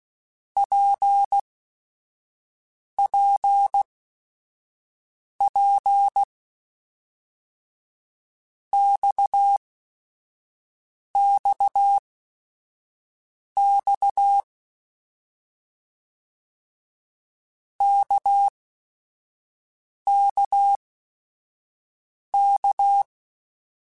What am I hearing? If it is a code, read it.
Morse code, heard as a keyed tone, about 16 wpm, spaced as for 5 wpm, 800 Hz. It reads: PPP XXX KKK
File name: Basic Sets / Basic Letter Groups Collective7 / RKPX@5